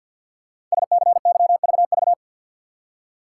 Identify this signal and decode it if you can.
Morse code, heard as a keyed tone, about 52 wpm, 700 Hz. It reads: SCYVV
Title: Code Groups - 5 Letters - 50wpm